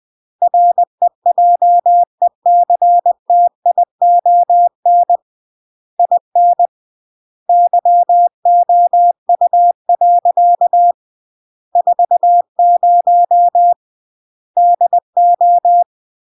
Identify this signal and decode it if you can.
Morse code, heard as a keyed tone, about 20 wpm, 700 Hz. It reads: REJECTION IN YOU. 40 DO
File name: CW / jodorowsky / 82 commandments 20-20